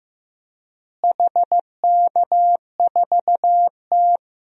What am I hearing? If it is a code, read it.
Morse code, heard as a keyed tone, about 15 wpm, 700 Hz. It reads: HK4T
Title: DXCC-100 Call Signs - 4 Characters - Repeated 2x - 15wpm